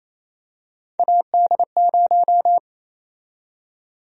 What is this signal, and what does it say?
Morse code, heard as a keyed tone, about 28 wpm, 700 Hz. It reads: AD0
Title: USA Call Sign Prefixes - 28wpm